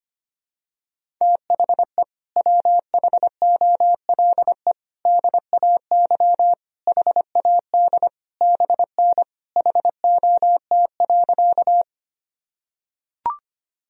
Morse code, heard as a keyed tone, about 25 wpm, 700 Hz. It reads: THE WHOLE DAY HAD BN HOT.
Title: Sentences from Top 500 Words and Abbreviations 25wpm